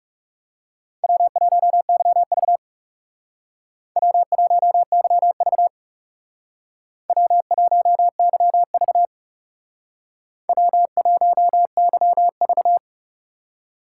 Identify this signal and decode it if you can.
Morse code, heard as a keyed tone, about 38 wpm, 700 Hz. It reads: W1YV W1YV W1YV W1YV